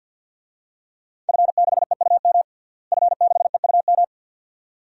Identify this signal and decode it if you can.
Morse code, heard as a keyed tone, about 53 wpm, 700 Hz. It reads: F6EUM F6EUM
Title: DXCC-100 Call Signs - 5 Characters - Repeated 2x - 50wpm